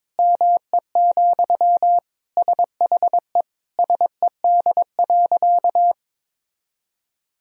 Morse code, heard as a keyed tone, about 21 wpm, 700 Hz. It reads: ME, SHE SED.
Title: Sentences from Top 500 Words and Abbreviations 22wpm